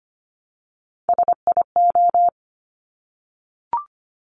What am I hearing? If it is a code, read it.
Morse code, heard as a keyed tone, about 25 wpm, 700 Hz. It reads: SIO
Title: Top Three Letter Combinations in Words 25wpm